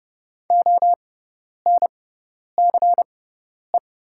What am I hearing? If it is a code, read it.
Morse code, heard as a keyed tone, about 30 wpm, spaced as for 10 wpm, 700 Hz. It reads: ONCE
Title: 6x Character Spacing - Top 500 Words - 30wpm